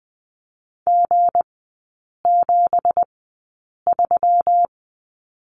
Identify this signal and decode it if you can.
Morse code, heard as a keyed tone, about 20 wpm, spaced as for 8 wpm, 700 Hz. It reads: G73
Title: 198 Lesson 32e - Letter k - Three Characters 20wpm